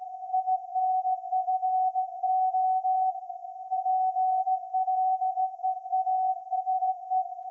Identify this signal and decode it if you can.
Morse code, heard as a keyed tone, about 16 wpm, 750 Hz. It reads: INFO PLEASE
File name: iz7slz / VLF / SAQ 30-06-2024 0900